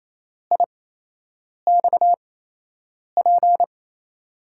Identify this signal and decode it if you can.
Morse code, heard as a keyed tone, about 28 wpm, spaced as for 8 wpm, 700 Hz. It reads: IXP
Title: USA Call Signs - 6 Characters - 8x Character Spacing - 28wpm